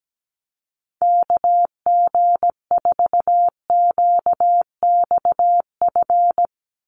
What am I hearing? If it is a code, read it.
Morse code, heard as a keyed tone, about 17 wpm, 700 Hz. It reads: KG4QXF
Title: USA Call Signs 17wpm